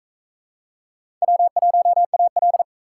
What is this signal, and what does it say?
Morse code, heard as a keyed tone, about 40 wpm, 700 Hz. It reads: W1AL